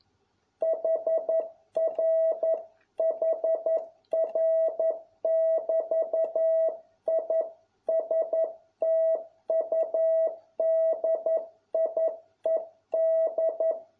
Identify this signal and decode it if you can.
Morse code, heard as a keyed tone, about 11 wpm, 650 Hz. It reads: HRHR<BT>I STUDIED